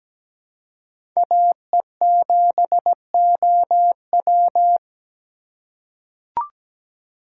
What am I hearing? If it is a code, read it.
Morse code, heard as a keyed tone, about 17 wpm, 700 Hz. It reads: AE7OW
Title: USA Call Signs - 3x Character Spacing - 17wpm